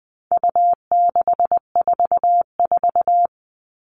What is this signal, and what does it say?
Morse code, heard as a keyed tone, about 20 wpm, 700 Hz. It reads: U644
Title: All Cut Numbers - 4-Digit 20wpm